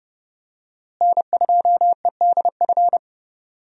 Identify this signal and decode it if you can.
Morse code, heard as a keyed tone, about 30 wpm, 700 Hz. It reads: N2EDF
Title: USA Call Signs - 5 Characters - Repeated 3x - 30wpm